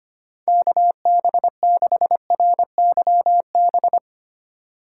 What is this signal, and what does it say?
Morse code, heard as a keyed tone, about 25 wpm, 700 Hz. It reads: KB6RYB
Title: USA Call Signs - 2x Character Spacing - 25wpm